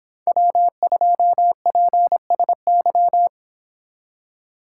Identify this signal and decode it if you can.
Morse code, heard as a keyed tone, about 25 wpm, 700 Hz. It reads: W2PSY